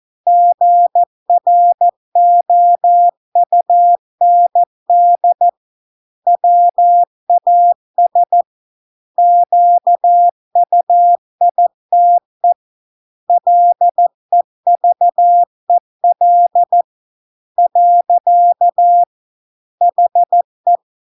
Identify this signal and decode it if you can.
Morse code, heard as a keyed tone, about 14 wpm, 700 Hz. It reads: GROUND WAS QUITE LEVEL . HE